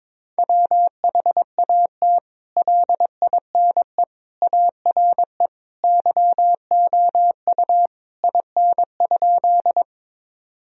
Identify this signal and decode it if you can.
Morse code, heard as a keyed tone, about 22 wpm, 700 Hz. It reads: WHAT LINE ARE YOU IN?
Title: Sentences from Top 1000 Words - Review Part 1 - 22wpm